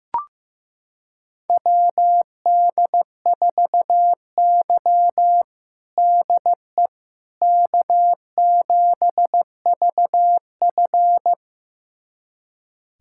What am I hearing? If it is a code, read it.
Morse code, heard as a keyed tone, about 16 wpm, 700 Hz. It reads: WD4Y DE K7VF
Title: USA Callsign DE Callsign - 4 Characters - 15wpm